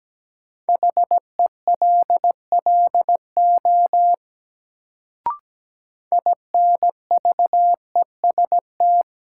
Morse code, heard as a keyed tone, about 17 wpm, 700 Hz. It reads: HELLO INVEST
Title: Sentences from Top 1100 Words - New Word Drill - 17wpm